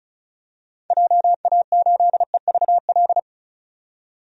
Morse code, heard as a keyed tone, about 34 wpm, 700 Hz. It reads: JA8EVL
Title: DXCC-100 Call Signs - 6 Characters - 35wpm